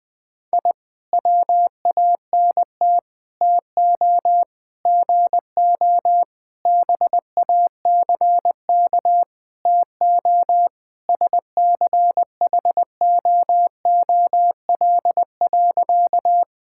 Morse code, heard as a keyed tone, about 20 wpm, 700 Hz. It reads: I WANT TO GO BACK TO SCHOOL.
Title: Sentences with 4-Word Phrases - 20wpm